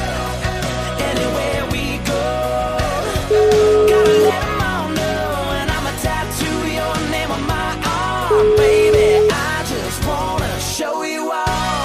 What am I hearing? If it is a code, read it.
Ringback tone (Europe, 425 Hz), with noise.